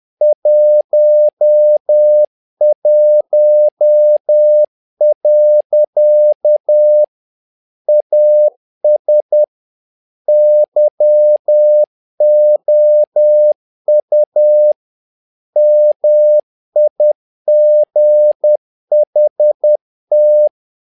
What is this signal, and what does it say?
Morse code, heard as a keyed tone, about 10 wpm, 600 Hz. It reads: 11. AS YOU MIGHT
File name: learning center / tower trouble / tower trouble 10wpm